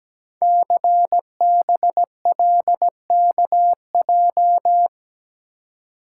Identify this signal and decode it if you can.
Morse code, heard as a keyed tone, about 17 wpm, 700 Hz. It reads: CBLKJ
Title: Code Groups - 5 Letters - 17wpm